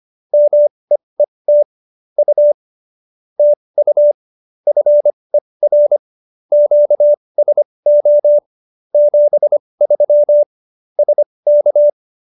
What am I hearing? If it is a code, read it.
Morse code, heard as a keyed tone, about 25 wpm, spaced as for 20 wpm, 590 Hz. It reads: MEET U TU FER QSO 73 SK
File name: qso207b 20